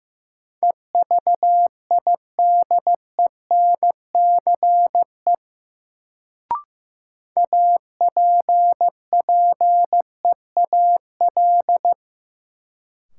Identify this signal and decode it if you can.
Morse code, heard as a keyed tone, about 15 wpm, 700 Hz. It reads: EVIDENCE APPEAL
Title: Sentences from Top 1300 Words - New Word Drill - 15wpm